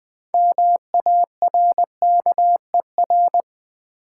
Morse code, heard as a keyed tone, about 20 wpm, 700 Hz. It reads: MARKER
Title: 6 Letter Words - First 3 Letters Given - 20wpm